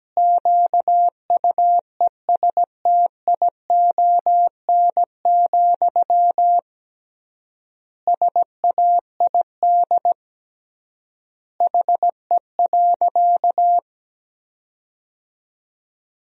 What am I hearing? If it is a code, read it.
Morse code, heard as a keyed tone, about 17 wpm, 700 Hz. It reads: QUESTION, SAID HE.
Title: 3x Word-Spacing - Sentences from Top 400 Words - 17wpm